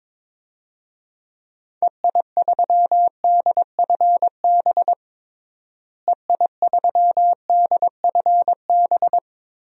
Morse code, heard as a keyed tone, about 22 wpm, 700 Hz. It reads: EI3DFB EI3DFB
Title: DXCC-100 Call Signs - 6 Characters - Repeated 3x - 22wpm